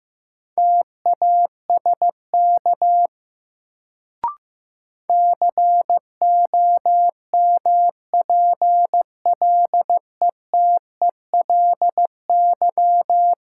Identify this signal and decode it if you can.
Morse code, heard as a keyed tone, about 15 wpm, 700 Hz. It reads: TASK COMPLETELY
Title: Sentences from Top 1100 Words - New Word Drill Alt - 15wpm